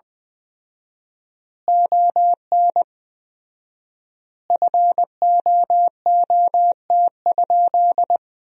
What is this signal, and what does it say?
Morse code, heard as a keyed tone, about 21 wpm, 700 Hz. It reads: ON FOOT?
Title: 4x Word-Spacing - Sentences from Top 500 Words - 20wpm